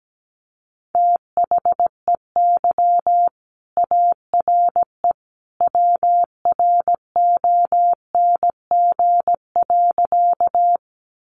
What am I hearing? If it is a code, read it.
Morse code, heard as a keyed tone, about 17 wpm, 700 Hz. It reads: THEY ARE WRONG.
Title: Sentences from Top 900 Words 17wpm